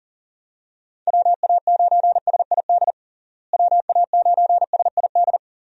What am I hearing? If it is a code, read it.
Morse code, heard as a keyed tone, about 37 wpm, 700 Hz. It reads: WA9SID WA9SID
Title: USA Call Signs - 6 Characters - Repeated 2x - 40wpm